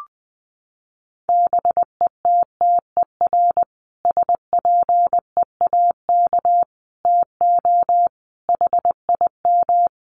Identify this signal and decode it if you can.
Morse code, heard as a keyed tone, about 20 wpm, 700 Hz. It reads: BETTER SPEAK TO HIM
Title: Sentences from Top 1100 Words 20wpm